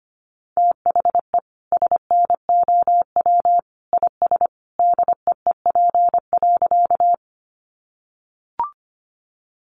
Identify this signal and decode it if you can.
Morse code, heard as a keyed tone, about 25 wpm, 700 Hz. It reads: THE SNOW IS DEEP.
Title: Sentences from Top 500 Words 25wpm